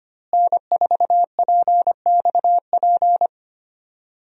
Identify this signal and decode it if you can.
Morse code, heard as a keyed tone, about 25 wpm, 700 Hz. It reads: N4PXP
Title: USA Call Signs - 5 Characters - Repeated 3x - 25wpm